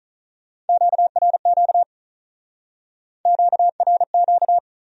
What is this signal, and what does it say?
Morse code, heard as a keyed tone, about 45 wpm, 700 Hz. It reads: QRQ QRQ